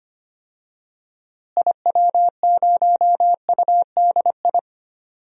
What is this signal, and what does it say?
Morse code, heard as a keyed tone, about 25 wpm, 700 Hz. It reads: IW0UDI
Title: DXCC-100 Call Signs - 6 Characters - Repeated 2x - 25wpm